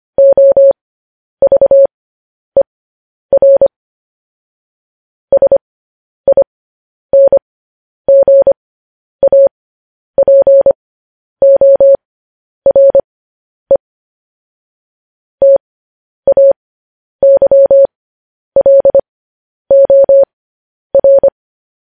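Morse code, heard as a keyed tone, about 25 wpm, spaced as for 10 wpm, 550 Hz. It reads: OVER SINGAPORE TAYLOR
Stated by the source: Morse code transcription of news headlines sent at 10 wpm.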